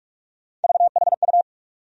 Morse code, heard as a keyed tone, about 46 wpm, 700 Hz. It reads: VHU
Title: USA Call Sign Postfixes - 3 Letters 45wpm